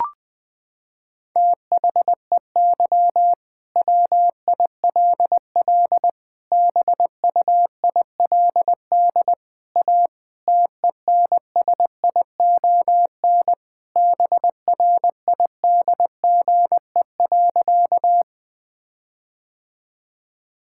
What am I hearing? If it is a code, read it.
Morse code, heard as a keyed tone, about 20 wpm, 700 Hz. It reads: THEY WILL BUILD A TENSION BRIDGE.
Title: Sentences from Top 1600 Words 20wpm